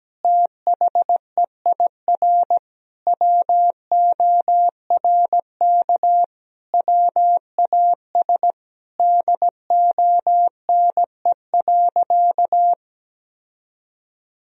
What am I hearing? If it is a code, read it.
Morse code, heard as a keyed tone, about 17 wpm, 700 Hz. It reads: THEIR WORK WAS DONE.
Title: Sentences from Top 1000 Words - Review Part 1 - 17wpm